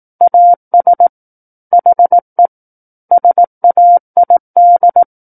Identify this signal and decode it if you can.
Morse code, heard as a keyed tone, about 18 wpm, 700 Hz. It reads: AS HE SAID
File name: 18 WPM